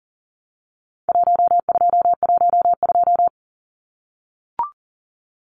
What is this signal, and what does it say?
Morse code, heard as a keyed tone, about 40 wpm, 700 Hz. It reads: 1212
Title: Numbers - Contest Style - 4-Digit 40wpm